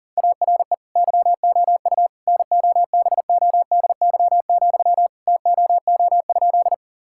Morse code, heard as a keyed tone, about 40 wpm, 700 Hz. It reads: ARE YOU NOBODY, TOO?
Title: Sentences from Top 1600 Words 40wpm